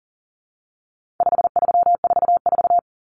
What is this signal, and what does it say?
Morse code, heard as a keyed tone, about 40 wpm, 700 Hz. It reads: 5344
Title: Common Cut Numbers - 4-Digit 40wpm